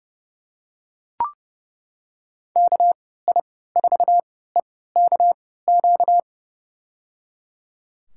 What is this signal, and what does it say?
Morse code, heard as a keyed tone, about 30 wpm, spaced as for 17 wpm, 700 Hz. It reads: KI4EKQ
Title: USA Call Signs - 6 Characters - 3x Character Spacing - 30wpm